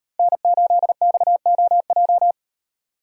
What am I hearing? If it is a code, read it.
Morse code, heard as a keyed tone, about 38 wpm, 700 Hz. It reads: N8XOJ